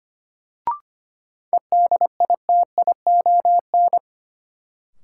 Morse code, heard as a keyed tone, about 25 wpm, 700 Hz. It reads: EDITION
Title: Sets of 5 Words - New Word Drill - 25wpm